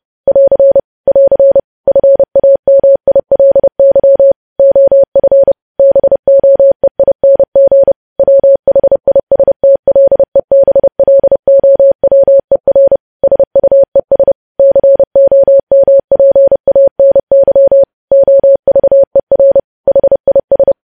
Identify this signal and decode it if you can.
Morse code, heard as a keyed tone, about 30 wpm, 550 Hz. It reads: <AR> <AR> FAMILY OF BOEING WHISTLEBLOWER SUES COMPANY OVER HIS
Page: news headlines in morse code at 30 wpm